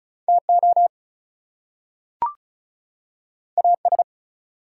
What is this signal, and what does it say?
Morse code, heard as a keyed tone, about 36 wpm, 700 Hz. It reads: TO AS
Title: Two Letter Words - 35wpm